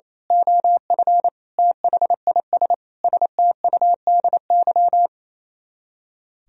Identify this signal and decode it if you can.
Morse code, heard as a keyed tone, about 27 wpm, 700 Hz. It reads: OF THIS STUDY
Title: Speed-Racing - Sets of 5 Words 28wpm